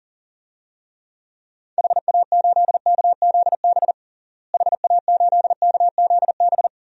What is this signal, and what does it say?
Morse code, heard as a keyed tone, about 40 wpm, 700 Hz. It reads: HA8KZB HA8KZB
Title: DXCC-100 Call Signs - 6 Characters - Repeated 3x - 40wpm